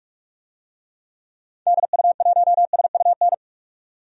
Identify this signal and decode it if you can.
Morse code, heard as a keyed tone, about 42 wpm, 700 Hz. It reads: DU1SUN
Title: DXCC-100 Call Signs - 6 Characters - 45wpm